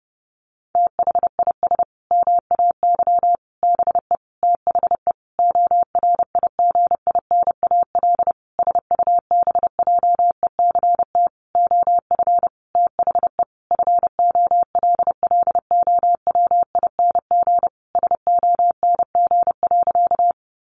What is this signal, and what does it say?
Morse code, heard as a keyed tone, about 30 wpm, 700 Hz. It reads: THIS MAY BE THE ORIGINAL SUBJECT OF THE FOLLOWING SONG.
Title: Sentences from Top 900 Words 30wpm